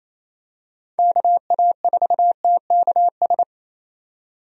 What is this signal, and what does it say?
Morse code, heard as a keyed tone, about 28 wpm, 700 Hz. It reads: KA4TKS
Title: USA Call Signs - 6 Characters - 28wpm